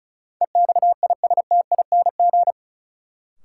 Morse code, heard as a keyed tone, about 36 wpm, 700 Hz. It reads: EXISTING
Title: Sentences from Top 1500 Words - New Word Drill - 35wpm